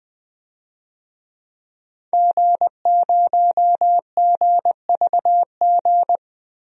Morse code, heard as a keyed tone, about 20 wpm, 700 Hz. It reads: G0GVG
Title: DXCC-100 Call Signs - 5 Characters - 20wpm